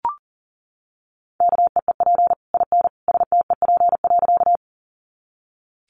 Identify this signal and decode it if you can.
Morse code, heard as a keyed tone, about 40 wpm, 700 Hz. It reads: KEEP IN STEP.
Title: Sentences from Top 400 Words 40wpm